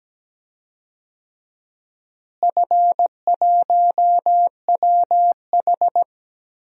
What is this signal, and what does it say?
Morse code, heard as a keyed tone, about 17 wpm, 700 Hz. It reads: F1WH
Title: DXCC-100 Call Signs - 4 Characters - Repeated 3x - 17wpm